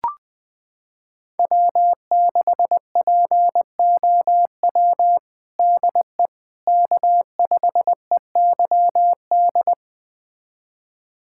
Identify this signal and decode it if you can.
Morse code, heard as a keyed tone, about 21 wpm, 700 Hz. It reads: W6POW DE K5EYD
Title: USA Callsign DE Callsign 20wpm